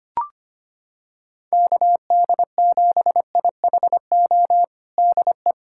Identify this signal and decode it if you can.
Morse code, heard as a keyed tone, about 25 wpm, 700 Hz. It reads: KD7IHO DE
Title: USA Callsign DE Callsign 25wpm